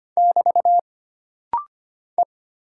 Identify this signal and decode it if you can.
Morse code, heard as a keyed tone, about 25 wpm, 700 Hz. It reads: <BT> E
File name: Single Character (Letter:Number:Punctuation:Prosign) - Rapid-Fire - 25wpm